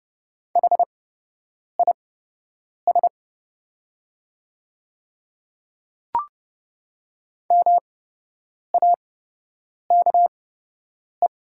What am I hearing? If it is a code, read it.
Morse code, heard as a keyed tone, about 30 wpm, spaced as for 8 wpm, 700 Hz. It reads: HIS MAKE
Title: 8x Character Spacing - Top 200 Words - 30wpm